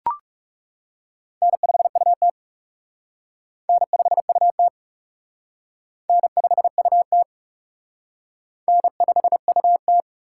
Morse code, heard as a keyed tone, about 36 wpm, 700 Hz. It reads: N5UT N5UT N5UT N5UT
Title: Speed Racing - USA 1x2 Call Signs 30wpm